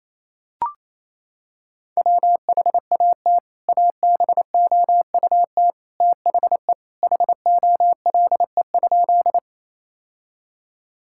Morse code, heard as a keyed tone, about 28 wpm, 700 Hz. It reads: WHAT ABOUT THE HOLE?
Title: Sentences from Top 1600 Words 28wpm